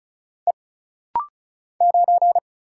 Morse code, heard as a keyed tone, about 35 wpm, 700 Hz. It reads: E 9
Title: Single Letter-Number Alt - Rapid-Fire 35wpm